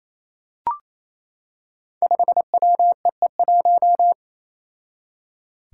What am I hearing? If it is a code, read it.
Morse code, heard as a keyed tone, about 28 wpm, 700 Hz. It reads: 5WEE1
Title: Five Challenging Letter-Number Combinations 28wpm